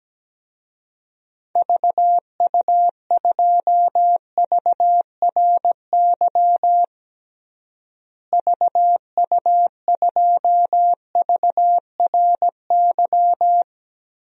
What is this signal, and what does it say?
Morse code, heard as a keyed tone, about 17 wpm, 700 Hz. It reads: VU2VRY VU2VRY
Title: DXCC-100 Call Signs - 6 Characters - Repeated 2x - 17wpm